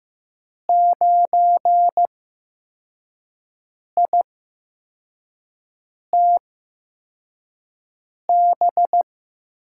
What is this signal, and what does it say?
Morse code, heard as a keyed tone, about 15 wpm, spaced as for 4 wpm, 700 Hz. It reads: 9ITB
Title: USA Call Signs - 5 Characters - 8x Character Spacing - 15wpm